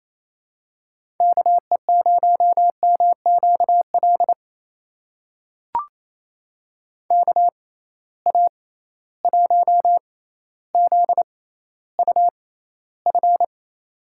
Morse code, heard as a keyed tone, about 28 wpm, spaced as for 14 wpm, 700 Hz. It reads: KE0MQL KA1ZUF
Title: USA Call Signs - 6 Characters - 6x Character Spacing - 28wpm